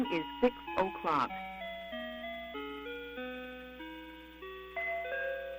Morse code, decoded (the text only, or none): none